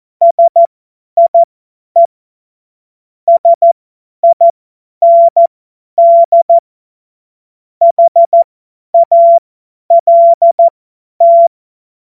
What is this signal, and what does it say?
Morse code, heard as a keyed tone, about 14 wpm, spaced as for 10 wpm, 680 Hz. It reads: SIE SIND HALT
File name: Morsekurs / Witze / Witz 09 Januar 2024 50 F70